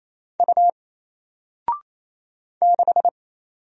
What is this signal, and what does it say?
Morse code, heard as a keyed tone, about 27 wpm, 700 Hz. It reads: U 6
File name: Single-Letter-Number-28wpm